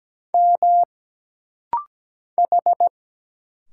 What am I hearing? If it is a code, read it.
Morse code, heard as a keyed tone, about 17 wpm, 700 Hz. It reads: M H
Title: Single Letter-Number Alt - Rapid-Fire 17wpm